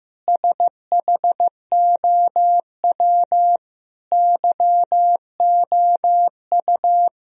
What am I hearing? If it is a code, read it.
Morse code, heard as a keyed tone, about 15 wpm, 700 Hz. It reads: SHOW YOU
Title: Sentences from Top 1000 Words - Review Part 3 - 15wpm